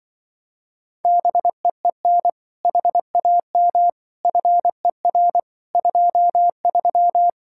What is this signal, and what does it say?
Morse code, heard as a keyed tone, about 24 wpm, 700 Hz. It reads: BEEN HAM FER 23
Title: Speed-Racing Common QSO Phrases Encore - 20wpm